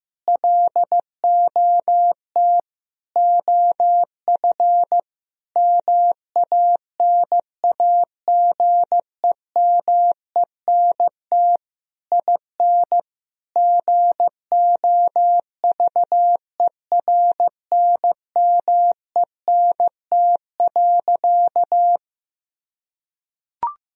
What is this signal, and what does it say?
Morse code, heard as a keyed tone, about 15 wpm, 700 Hz. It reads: LOT OF MANAGEMENT IN GOVERNMENT.
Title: Sentences from Top 1000 Words - Review Part 4 - 15wpm